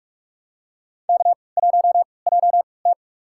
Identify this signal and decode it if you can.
Morse code, heard as a keyed tone, about 45 wpm, spaced as for 25 wpm, 700 Hz. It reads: K1JT
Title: USA Call Signs - 3x Character Spacing - 45wpm